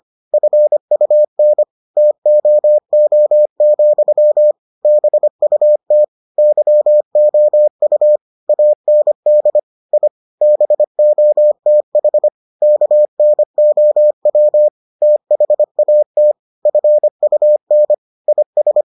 Morse code, heard as a keyed tone, about 25 wpm, 600 Hz. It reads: FUN TOO, BUT YOU AND I BOTH KNOW THAT FUN IS